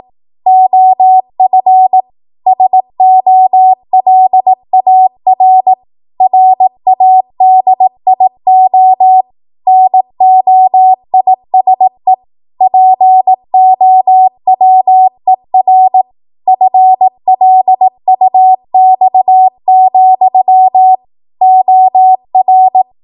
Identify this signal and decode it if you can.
Morse code, heard as a keyed tone, about 18 wpm, 750 Hz. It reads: OF SOLAR RADIO NOISE POWER FLUX, OR
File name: Morse / 18WPM